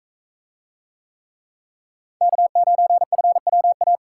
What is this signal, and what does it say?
Morse code, heard as a keyed tone, about 40 wpm, 700 Hz. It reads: K9FWA